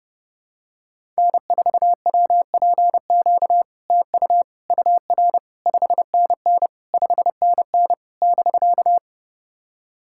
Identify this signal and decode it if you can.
Morse code, heard as a keyed tone, about 30 wpm, 690 Hz. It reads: N4WPQ TU UR 5NN 5NN <BK>
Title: SOTA QSO Practice 30wpm